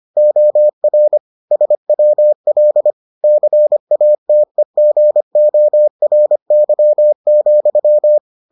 Morse code, heard as a keyed tone, about 25 wpm, 600 Hz. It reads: OR SWL CATEGORY,